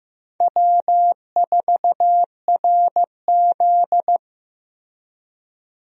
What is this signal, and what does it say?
Morse code, heard as a keyed tone, about 15 wpm, 700 Hz. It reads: W4RZ
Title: USA Call Signs - 4 Characters - Repeated 3x - 15wpm